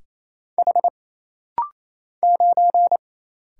Morse code, heard as a keyed tone, about 27 wpm, 700 Hz. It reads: H 9
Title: Single Letter-Number - Rapid-Fire 28wpm